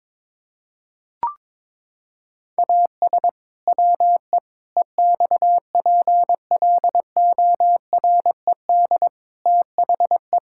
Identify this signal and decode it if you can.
Morse code, heard as a keyed tone, about 22 wpm, 700 Hz. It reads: AS WE EXPLORED THE
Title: Pangram Challenge v2 - 22wpm